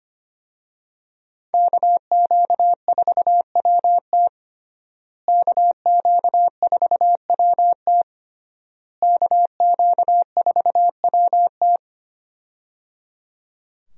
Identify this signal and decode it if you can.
Morse code, heard as a keyed tone, about 25 wpm, 700 Hz. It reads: KQ4WT KQ4WT KQ4WT
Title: USA Call Signs - 5 Characters - Repeated 3x - 25wpm